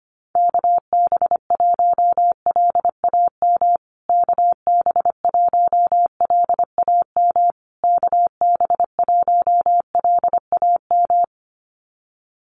Morse code, heard as a keyed tone, about 25 wpm, 700 Hz. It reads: KB1LAM KB1LAM KB1LAM
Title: USA Call Signs Repeated 3x 25wpm